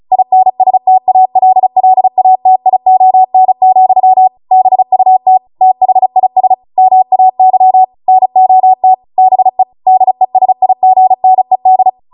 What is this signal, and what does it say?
Morse code, heard as a keyed tone, about 36 wpm, 750 Hz. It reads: INSTALLATION, BUT THIS MAY NOT BE DESIGNED